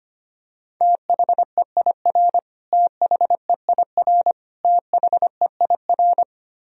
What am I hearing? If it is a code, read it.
Morse code, heard as a keyed tone, about 25 wpm, 700 Hz. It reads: THEIR THEIR THEIR
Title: Top 100 Words Repeated 25wpm